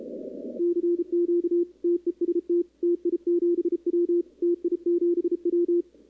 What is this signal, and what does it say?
Morse code, heard as a keyed tone, about 34 wpm, 340 Hz, with radio noise. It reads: CQ TEST TI7W TI7W
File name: contests / 2019-cw-wpx-kt5j-k5pi- -2330 0-11